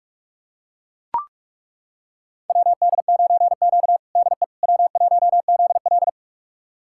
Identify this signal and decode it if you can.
Morse code, heard as a keyed tone, about 41 wpm, 700 Hz. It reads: WD9Q DE W1ZL
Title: USA Callsign DE Callsign - 4 Characters - 45wpm